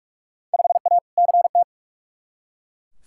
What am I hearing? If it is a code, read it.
Morse code, heard as a keyed tone, about 41 wpm, 700 Hz. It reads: 5A CT
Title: Field Day Exchanges - 45wpm